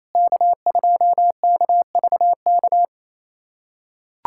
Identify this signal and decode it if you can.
Morse code, heard as a keyed tone, about 27 wpm, 700 Hz. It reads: K2KVK
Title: USA Callsign DE Callsign 28wpm